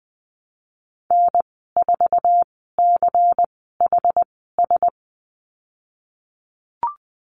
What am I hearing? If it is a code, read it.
Morse code, heard as a keyed tone, about 20 wpm, spaced as for 15 wpm, 700 Hz. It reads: N4CHS
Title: 128 Lesson 21g - Number 6 - Call Signs 20wpm